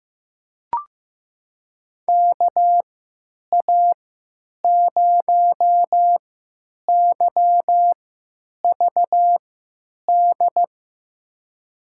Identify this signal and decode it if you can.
Morse code, heard as a keyed tone, about 15 wpm, spaced as for 9 wpm, 700 Hz. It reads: KA0YVD
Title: USA Call Signs - 3x Character Spacing - 15wpm